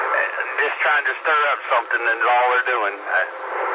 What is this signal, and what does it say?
Emergency Alert System (SAME) tones, heard as harsh buzzing bursts, with speech. It is heard from 0.7 to 3.3 s.